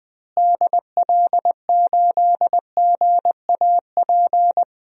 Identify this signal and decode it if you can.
Morse code, heard as a keyed tone, about 20 wpm, 700 Hz. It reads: DL8GAP
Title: DXCC-100 Call Signs - 6 Characters - Repeated 2x - 20wpm